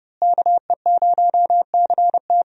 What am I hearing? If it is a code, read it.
Morse code, heard as a keyed tone, about 30 wpm, 700 Hz. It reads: KE0CT
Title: USA Call Signs - 5 Characters - Repeated 3x - 30wpm